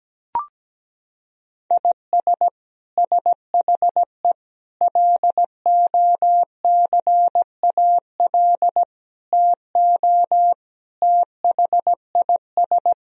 Morse code, heard as a keyed tone, about 17 wpm, 700 Hz. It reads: IS SHE LOCAL TO THIS
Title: Sentences from Top 1000 Words - Review Part 3 - 17wpm